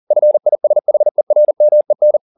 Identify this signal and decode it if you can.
Morse code, heard as a keyed tone, about 40 wpm, 600 Hz. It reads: FISHERMEN